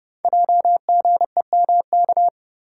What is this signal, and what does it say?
Morse code, heard as a keyed tone, about 30 wpm, 700 Hz. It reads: JGEMK
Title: Code Groups - 5 Letters - 30wpm